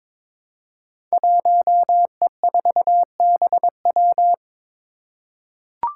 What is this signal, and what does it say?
Morse code, heard as a keyed tone, about 23 wpm, 700 Hz. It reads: 1E4BW
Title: Five Challenging Letter-Number Combinations 22wpm